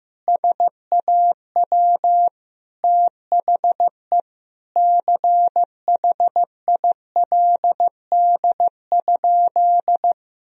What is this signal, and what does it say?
Morse code, heard as a keyed tone, about 15 wpm, 700 Hz. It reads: SAW THE CHILD?
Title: Sentences from Top 1000 Words - Review Part 1 - 15wpm